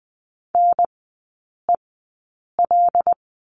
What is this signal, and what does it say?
Morse code, heard as a keyed tone, about 20 wpm, spaced as for 8 wpm, 700 Hz. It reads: NEL
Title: 127 Lesson 21e - Number 6 - Three Characters 20wpm